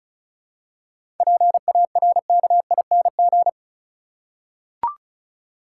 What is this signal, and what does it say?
Morse code, heard as a keyed tone, about 34 wpm, 700 Hz. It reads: PARKING
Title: Seven Letter Words - 35wpm